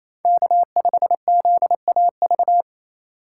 Morse code, heard as a keyed tone, about 27 wpm, 700 Hz. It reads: K5ZAV
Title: USA Call Signs - 8x Character Spacing - 28wpm